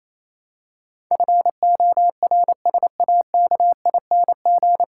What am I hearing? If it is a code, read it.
Morse code, heard as a keyed tone, about 28 wpm, 700 Hz. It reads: FORSAKING
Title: Pangram Challenge v2 - 28wpm